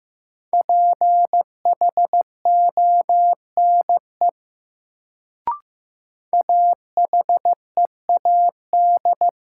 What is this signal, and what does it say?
Morse code, heard as a keyed tone, about 15 wpm, 700 Hz. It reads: PHONE AHEAD
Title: Sentences from Top 1100 Words - New Word Drill Alt - 15wpm